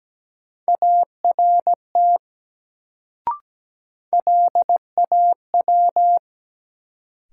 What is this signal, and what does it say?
Morse code, heard as a keyed tone, about 17 wpm, 700 Hz. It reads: ART LAW
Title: Sentences from Top 600 Words - New Word Drill - 17wpm